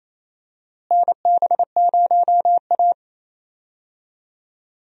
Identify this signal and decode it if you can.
Morse code, heard as a keyed tone, about 28 wpm, 700 Hz. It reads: NB0A
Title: USA Call Signs - 4 Characters - Repeated 3x - 28wpm